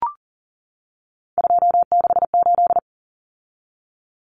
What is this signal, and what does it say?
Morse code, heard as a keyed tone, about 40 wpm, 700 Hz. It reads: 268
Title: Common Cut Numbers - 3-Digit 40wpm